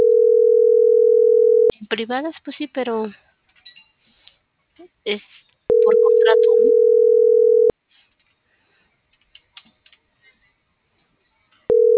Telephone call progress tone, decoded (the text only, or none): ringback tone